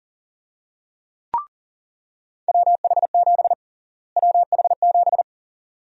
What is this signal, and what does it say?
Morse code, heard as a keyed tone, about 40 wpm, 700 Hz. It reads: WH7 WH7
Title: USA Call Sign Prefixes - Repeated 2x - 40wpm